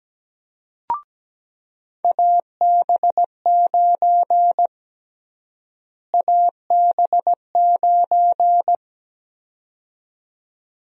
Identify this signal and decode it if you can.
Morse code, heard as a keyed tone, about 17 wpm, 700 Hz. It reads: AB9 AB9
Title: USA Call Sign Prefixes - Repeated 2x - 17wpm